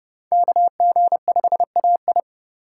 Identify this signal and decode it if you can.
Morse code, heard as a keyed tone, about 29 wpm, 700 Hz. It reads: KG5AI